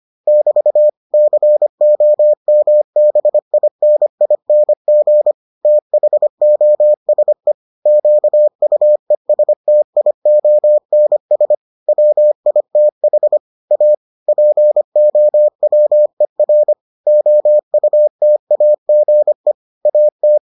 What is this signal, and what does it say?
Morse code, heard as a keyed tone, about 25 wpm, 600 Hz. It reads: <BT> COMBINING THOSE QUESTIONS WITH A POWER OUTAGE AT